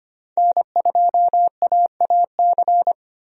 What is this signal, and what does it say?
Morse code, heard as a keyed tone, about 25 wpm, 700 Hz. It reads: N2AAC
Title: USA Call Signs - 5 Characters - 25wpm